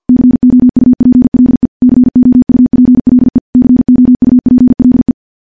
Morse code, heard as a keyed tone, about 50 wpm, 250 Hz. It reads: COUPLE COUPLE COUPLE
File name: morse / morse-training-50WPM